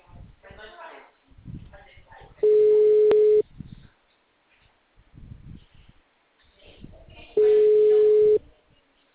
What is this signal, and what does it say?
Ringback tone (Europe, 425 Hz), with speech.
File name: out- -2014- - - .539